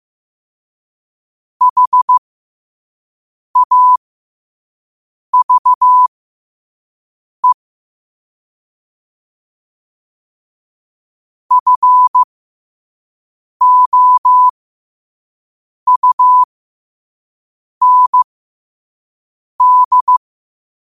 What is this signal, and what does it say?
Morse code, heard as a keyed tone, about 15 wpm, spaced as for 5 wpm, 1000 Hz. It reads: HAVE FOUND
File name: qotdfiles / 5wpm